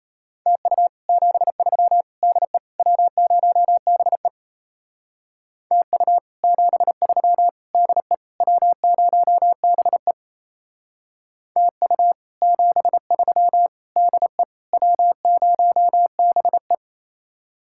Morse code, heard as a keyed tone, about 35 wpm, 700 Hz. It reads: TU 73 DE W0BE TU 73 DE W0BE TU 73 DE W0BE